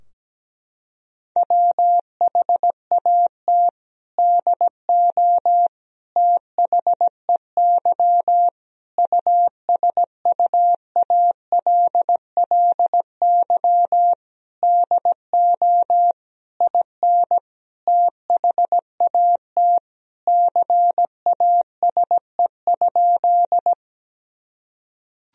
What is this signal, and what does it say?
Morse code, heard as a keyed tone, about 17 wpm, 700 Hz. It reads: WHAT DO THEY USUALLY DO IN THAT CASE?
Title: Sentences from Top 1000 Words - Review Part 3 - 17wpm